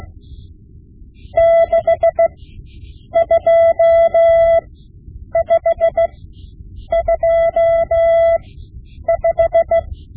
Morse code, heard as a keyed tone, about 16 wpm, spaced as for 8 wpm, 670 Hz. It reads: 62525
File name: Morse Lessons / Morse Lessons / Tape-5B